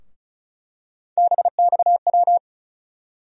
Morse code, heard as a keyed tone, about 34 wpm, 700 Hz. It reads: BXW
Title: USA Call Sign Postfixes - 3 Letters 35wpm